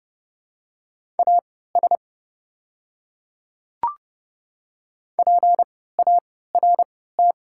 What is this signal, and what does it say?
Morse code, heard as a keyed tone, about 32 wpm, spaced as for 17 wpm, 700 Hz. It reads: AS PART
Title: 3x Character Spacing - Top 300 Words - 30wpm